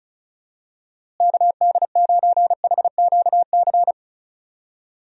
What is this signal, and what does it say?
Morse code, heard as a keyed tone, about 34 wpm, 700 Hz. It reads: KD9HQC
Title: USA Call Signs - 6 Characters - Repeated 3x - 35wpm